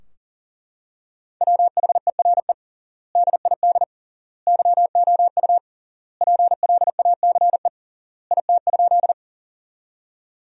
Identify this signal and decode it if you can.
Morse code, heard as a keyed tone, about 40 wpm, 700 Hz. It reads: WHERE DID YOU PLACE IT?
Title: 3x Word-Spacing - Sentences from Top 200 Words - 40wpm